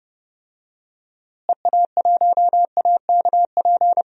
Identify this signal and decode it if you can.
Morse code, heard as a keyed tone, about 30 wpm, 700 Hz. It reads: EA1AKP
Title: DXCC-100 Call Signs - 6 Characters - Repeated 3x - 30wpm